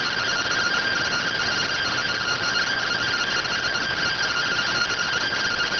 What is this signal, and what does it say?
Two-tone teleprinter signal, about 170 Hz shift.